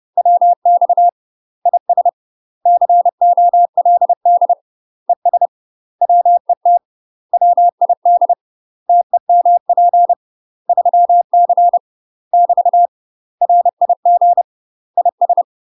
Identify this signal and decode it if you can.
Morse code, heard as a keyed tone, about 30 wpm, 700 Hz. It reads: WX IS COLD ES WET WID TEMP 3C <BT> RIG IS
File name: UK MASTER QSOs TRAINING / 30WPM / UK MASTER QSOs 0027